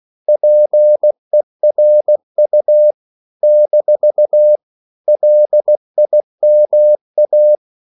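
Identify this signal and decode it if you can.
Morse code, heard as a keyed tone, about 16 wpm, 600 Hz. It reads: PERU - LIMA
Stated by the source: CW practice of national capitals at 16 words per minute, from ON6ZQ.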